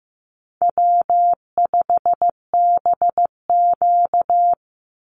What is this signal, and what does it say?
Morse code, heard as a keyed tone, about 15 wpm, 700 Hz. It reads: W5BQ
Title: Call Signs Repeated 3x - 15wpm